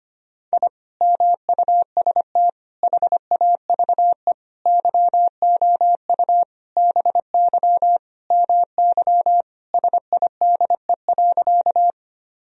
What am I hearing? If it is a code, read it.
Morse code, heard as a keyed tone, about 25 wpm, 700 Hz. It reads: I MUST HAVE YOU BY MY SIDE.
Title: Sentences from Top 1000 Words - Review Part 1 - 25wpm